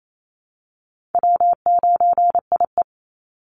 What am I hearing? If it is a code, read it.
Morse code, heard as a keyed tone, about 28 wpm, 700 Hz. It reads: W9IE
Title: Easy Call Signs Repeated 3x - 28wpm